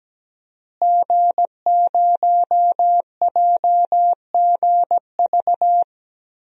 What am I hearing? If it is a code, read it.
Morse code, heard as a keyed tone, about 17 wpm, 700 Hz. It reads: G0JGV
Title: DXCC-100 Call Signs - 5 Characters - Repeated 3x - 17wpm